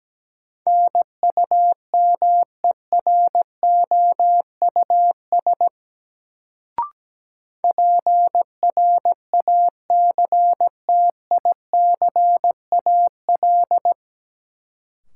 Morse code, heard as a keyed tone, about 17 wpm, 700 Hz. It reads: NUMEROUS PRACTICAL
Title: Sentences from Top 1400 Words - New Word Drill - 17wpm